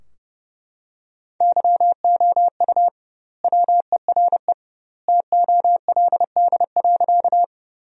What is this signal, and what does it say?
Morse code, heard as a keyed tone, about 30 wpm, 700 Hz. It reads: YOU WERE TOLD.
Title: 2x Word-Spacing - Sentences from Top 400 Words - 30wpm